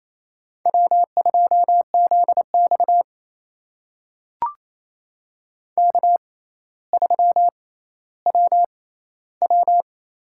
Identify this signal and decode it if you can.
Morse code, heard as a keyed tone, about 27 wpm, spaced as for 14 wpm, 700 Hz. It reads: W2ZX K3WW
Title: USA Call Signs - 4 Characters - 6x Character Spacing - 28wpm